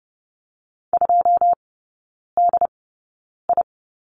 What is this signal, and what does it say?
Morse code, heard as a keyed tone, about 30 wpm, spaced as for 9 wpm, 700 Hz. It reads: 2DI
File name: Lesson 16e - Letter c - Three Characters 30wpm